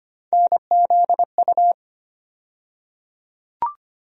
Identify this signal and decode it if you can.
Morse code, heard as a keyed tone, about 25 wpm, 700 Hz. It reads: NZU
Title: USA Call Sign Postfixes - 3 Letters 25wpm